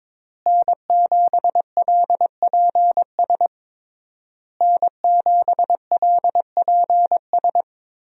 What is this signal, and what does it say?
Morse code, heard as a keyed tone, about 23 wpm, 700 Hz. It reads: N7LPS N7LPS
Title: USA Call Signs - 5 Characters - Repeated 3x - 22wpm